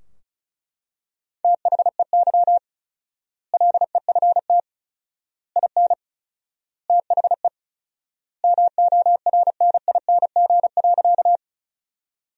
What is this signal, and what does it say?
Morse code, heard as a keyed tone, about 36 wpm, 700 Hz. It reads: THEY LEFT IN THE MORNING.
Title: 4x Word-Spacing - Sentences from Top 400 Words - 35wpm